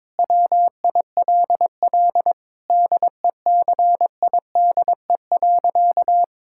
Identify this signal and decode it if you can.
Morse code, heard as a keyed tone, about 22 wpm, 700 Hz. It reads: WILL DECIDE.
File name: Sentences with 7-Word Phrases - 22wpm